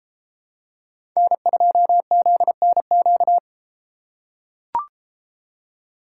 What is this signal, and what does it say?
Morse code, heard as a keyed tone, about 32 wpm, 700 Hz. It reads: N2ZNQ